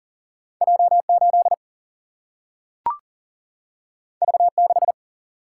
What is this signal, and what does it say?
Morse code, heard as a keyed tone, about 40 wpm, 700 Hz. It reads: J 8 V 6
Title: Single-Letter Single-Number 40wpm